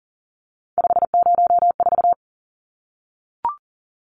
Morse code, heard as a keyed tone, about 40 wpm, 700 Hz. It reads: 504
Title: Numbers - 3-Digit 40wpm